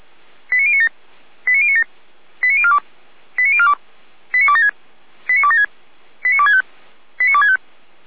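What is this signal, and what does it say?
Five-tone selective call (ZVEI), heard as quick runs of five tones, tones of about 70 ms. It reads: 89097 89097 89042 89042 89276 89276 89265 89265
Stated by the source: ZVEI five-tone alert received by SDR, fire brigade feuerwehr-orlamuende.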